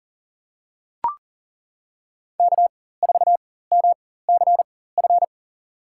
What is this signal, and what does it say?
Morse code, heard as a keyed tone, about 40 wpm, spaced as for 19 wpm, 700 Hz. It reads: K4MCF
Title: USA Call Signs - 5 Characters - 4x Character Spacing - 40wpm